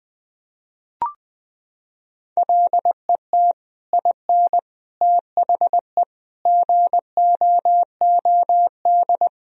Morse code, heard as a keyed tone, about 21 wpm, 700 Hz. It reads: LET IN THE GOOD
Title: Sentences from Top 1000 Words - Review Part 1 - 20wpm